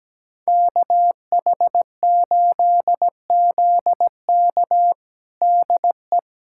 Morse code, heard as a keyed tone, about 17 wpm, 700 Hz. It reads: KH8ZK DE
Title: USA Callsign DE Callsign - 5 Characters - 17wpm